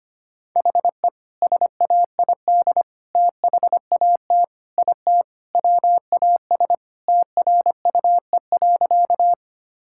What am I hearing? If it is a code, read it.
Morse code, heard as a keyed tone, about 25 wpm, 700 Hz. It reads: HE SAID THAT IT WAS TRUE.
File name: practice / 25wpm / Sentences with 2-Word Phrases - 25wpm